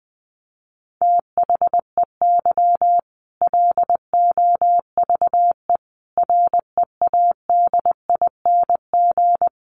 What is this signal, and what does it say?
Morse code, heard as a keyed tone, about 20 wpm, 700 Hz. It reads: THEY LOVE READING
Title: Sentences from Top 600 Words 20wpm